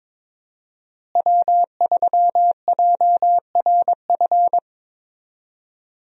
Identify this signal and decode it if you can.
Morse code, heard as a keyed tone, about 22 wpm, 700 Hz. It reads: W3JRF
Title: USA Call Signs - 5 Characters - Repeated 2x - 22wpm